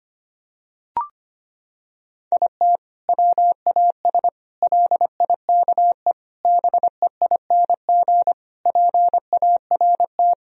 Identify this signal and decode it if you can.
Morse code, heard as a keyed tone, about 25 wpm, 700 Hz. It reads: IT WAS LIKE BEING PART
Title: Sentences from Top 1000 Words - Review Part 3 - 25wpm